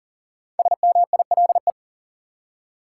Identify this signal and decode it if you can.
Morse code, heard as a keyed tone, about 37 wpm, 700 Hz. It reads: SMILE
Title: Five Letter Words - 40wpm